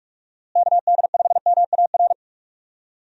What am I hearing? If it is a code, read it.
Morse code, heard as a keyed tone, about 46 wpm, 700 Hz. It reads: KD5MAR
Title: USA Call Signs - 6 Characters - Repeated 3x - 45wpm